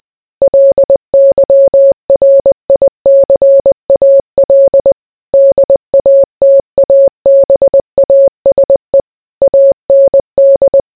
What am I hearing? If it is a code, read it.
Morse code, heard as a keyed tone, about 20 wpm, 550 Hz. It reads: LYRICAL DATABASE AND